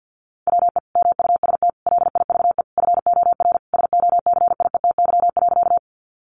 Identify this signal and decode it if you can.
Morse code, heard as a keyed tone, about 50 wpm, 700 Hz. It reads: WE MUST LIVE FOR SOCIETY.
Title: Sentences from Top 700 Words 50wpm